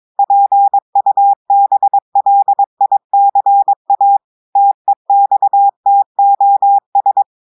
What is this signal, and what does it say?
Morse code, heard as a keyed tone, about 22 wpm, 800 Hz. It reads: PUBLICA TEXTOS
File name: audios / portugais / portugais texte en clair 1 vitesse 22 mots minutes